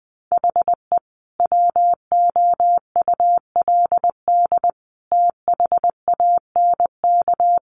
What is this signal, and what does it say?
Morse code, heard as a keyed tone, about 20 wpm, 700 Hz. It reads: HE WOULD THANK
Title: Sentences from Top 1100 Words 20wpm